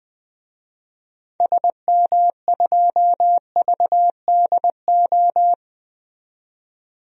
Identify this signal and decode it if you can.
Morse code, heard as a keyed tone, about 20 wpm, 700 Hz. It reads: SM2VDO
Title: DXCC-100 Call Signs - 6 Characters - 20wpm